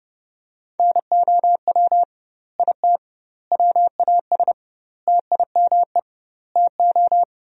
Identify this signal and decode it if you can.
Morse code, heard as a keyed tone, about 30 wpm, 700 Hz. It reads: NOW IT WAS TIME TO
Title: 2x Word-Spacing - Sentences from Top 200 Words - 30wpm